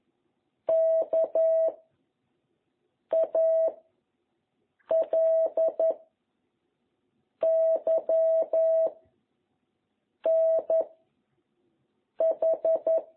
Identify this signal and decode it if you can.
Morse code, heard as a keyed tone, about 11 wpm, 660 Hz. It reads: K A L Y N H